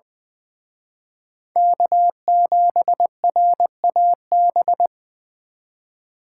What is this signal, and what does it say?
Morse code, heard as a keyed tone, about 24 wpm, 700 Hz. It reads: K7RAB
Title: Speed Racing - USA 1x3 Call Signs 20wpm